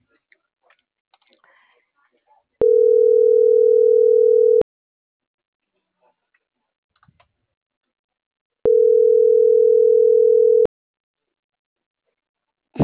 Ringback tone (North America), 440 + 480 Hz.